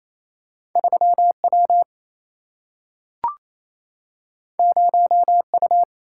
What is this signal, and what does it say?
Morse code, heard as a keyed tone, about 28 wpm, 700 Hz. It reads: 3 W 0 U
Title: Single-Letter Single-Number 28wpm